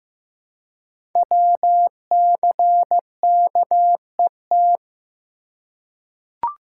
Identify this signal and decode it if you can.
Morse code, heard as a keyed tone, about 15 wpm, 700 Hz. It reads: WCKET
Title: Code Groups - 5 Letters - 15wpm